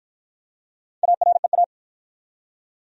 Morse code, heard as a keyed tone, about 50 wpm, 700 Hz. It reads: AREA